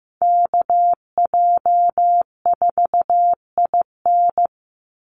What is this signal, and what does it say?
Morse code, heard as a keyed tone, about 15 wpm, 700 Hz. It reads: KJ4IN
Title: USA Call Signs 15wpm